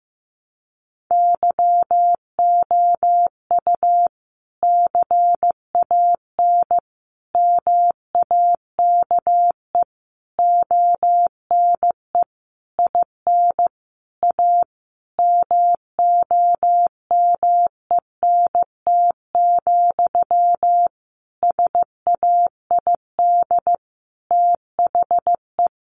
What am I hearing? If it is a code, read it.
Morse code, heard as a keyed tone, about 15 wpm, 700 Hz. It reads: YOU CAN MAKE ONE IN A MOMENT, SAID THE
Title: Sentences from Top 1000 Words 15wpm